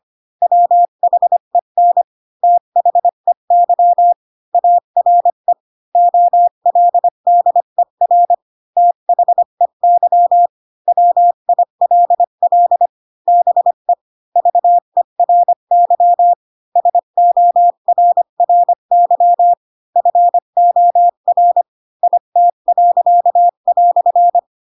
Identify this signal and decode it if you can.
Morse code, heard as a keyed tone, about 25 wpm, 700 Hz. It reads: WHEN THEY ARE OLDER THEY WILL BE VERY SORRY FOR IT."
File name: Pinocchio / Pinocchio ch4 20 25 ik1hge